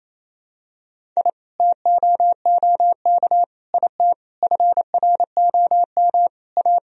Morse code, heard as a keyed tone, about 28 wpm, 700 Hz. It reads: I TOOK IT FROM A
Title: Sentences from Top 1000 Words - Review Part 1 - 28wpm